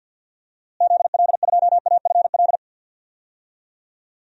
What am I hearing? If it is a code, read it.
Morse code, heard as a keyed tone, about 51 wpm, 700 Hz. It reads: ZL2AFL
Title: DXCC-100 Call Signs - 6 Characters - Repeated 2x - 50wpm